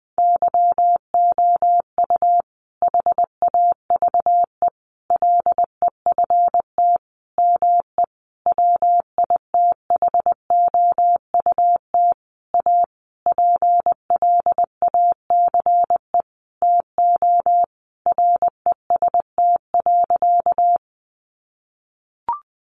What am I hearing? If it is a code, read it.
Morse code, heard as a keyed tone, about 20 wpm, 700 Hz. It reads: YOU HAVE LEFT ME WITHOUT A PLACE TO REST.
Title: Sentences from Top 600 Words 20wpm